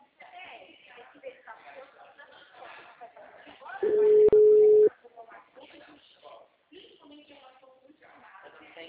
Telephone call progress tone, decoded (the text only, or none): ringback tone